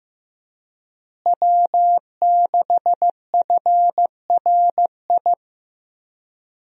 Morse code, heard as a keyed tone, about 16 wpm, 700 Hz. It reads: W6FRI
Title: USA Call Signs - 5 Characters - Repeated 3x - 15wpm